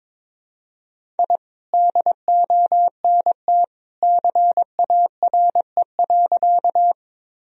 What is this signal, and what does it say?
Morse code, heard as a keyed tone, about 22 wpm, 700 Hz. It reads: I DONT CARE.
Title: Sentences from Top 1000 Words - Review Part 1 - 22wpm